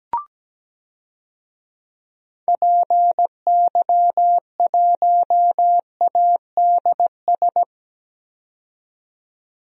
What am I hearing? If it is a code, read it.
Morse code, heard as a keyed tone, about 17 wpm, 700 Hz. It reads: PY1ADS
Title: DXCC-100 Call Signs - 6 Characters - 17wpm